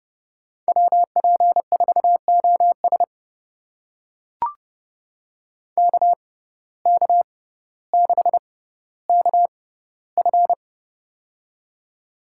Morse code, heard as a keyed tone, about 30 wpm, spaced as for 15 wpm, 700 Hz. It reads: WP4OS KK6KF
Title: USA Call Signs - 5 Characters - 6x Character Spacing - 30wpm